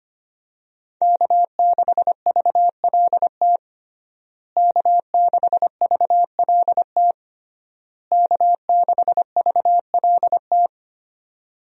Morse code, heard as a keyed tone, about 25 wpm, 700 Hz. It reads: K6VLT K6VLT K6VLT
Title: USA Call Signs - 5 Characters - Repeated 3x - 25wpm